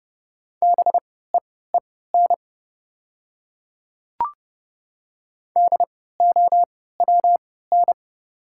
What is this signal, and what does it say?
Morse code, heard as a keyed tone, about 30 wpm, spaced as for 17 wpm, 690 Hz. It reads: BEEN DOWN
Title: 3x Character Spacing - Top 200 Words - 30wpm